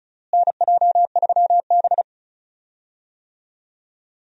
Morse code, heard as a keyed tone, about 34 wpm, 700 Hz. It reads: NJ3B
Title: USA Call Signs - 4 Characters - Repeated 3x - 35wpm